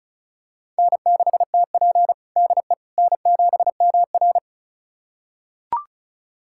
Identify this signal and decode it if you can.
Morse code, heard as a keyed tone, about 36 wpm, 700 Hz. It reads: N6TP DE N7MR
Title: USA Callsign DE Callsign - 4 Characters - 35wpm